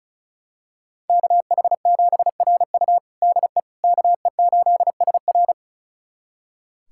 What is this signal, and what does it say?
Morse code, heard as a keyed tone, about 34 wpm, 700 Hz. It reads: KH7RU DE KE8SR
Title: USA Callsign DE Callsign - 5 Characters - 35wpm